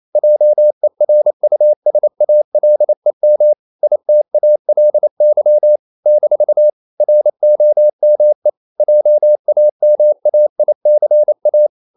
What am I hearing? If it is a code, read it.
Morse code, heard as a keyed tone, about 28 wpm, 600 Hz. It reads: JERUSALEM ITALY - ROME JAMAICA
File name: CW / Practice / NationalCapitals28